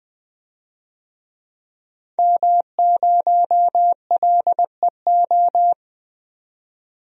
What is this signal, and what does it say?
Morse code, heard as a keyed tone, about 20 wpm, 700 Hz. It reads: M0LEO
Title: DXCC-100 Call Signs - 5 Characters - 20wpm